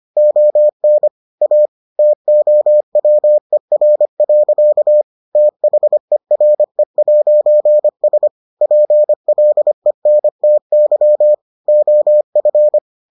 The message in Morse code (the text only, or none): ON A TOWER. THERE'S PLENTY OF